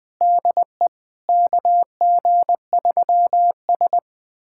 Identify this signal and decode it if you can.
Morse code, heard as a keyed tone, about 20 wpm, 700 Hz. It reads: DE KG3S
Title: USA Callsign DE Callsign - 4 Characters - 20wpm